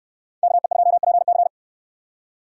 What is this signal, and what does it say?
Morse code, heard as a keyed tone, about 68 wpm, 700 Hz. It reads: KE2PC